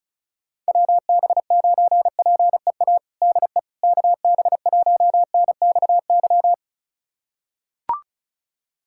Morse code, heard as a keyed tone, about 36 wpm, 700 Hz. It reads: WB9PEA DE KB1NXY
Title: USA Callsign DE Callsign - 6 Characters - 35wpm